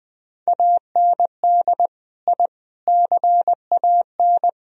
Morse code, heard as a keyed tone, about 20 wpm, 700 Hz. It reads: AND I CAN